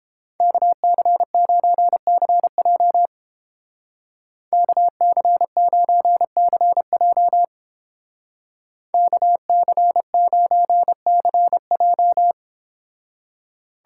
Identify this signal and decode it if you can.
Morse code, heard as a keyed tone, about 29 wpm, 700 Hz. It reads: KC9CJ KC9CJ KC9CJ